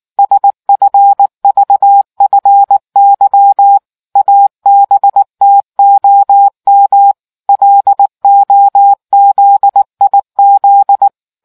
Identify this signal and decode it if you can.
Morse code, heard as a keyed tone, about 19 wpm, 800 Hz. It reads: SFVFY ABTOM LOZIZ